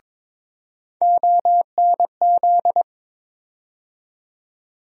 Morse code, heard as a keyed tone, about 22 wpm, 700 Hz. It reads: ONZ
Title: USA Call Sign Postfixes - 3 Letters 22wpm